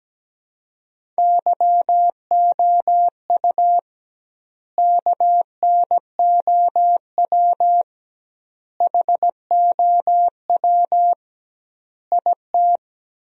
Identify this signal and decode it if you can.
Morse code, heard as a keyed tone, about 17 wpm, 700 Hz. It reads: YOU KNOW HOW IT
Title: 2x Word-Spacing - Sentences from Top 100 Words - 17wpm